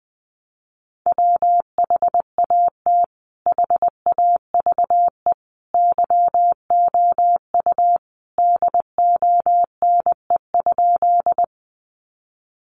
Morse code, heard as a keyed tone, about 20 wpm, 700 Hz. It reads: WHAT HAVE YOU DONE?
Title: Sentences from Top 500 Words 20wpm